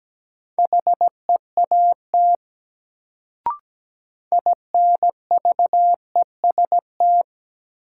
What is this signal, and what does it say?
Morse code, heard as a keyed tone, about 17 wpm, 700 Hz. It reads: HEAT INVEST
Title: Sentences from Top 1100 Words - New Word Drill Alt - 17wpm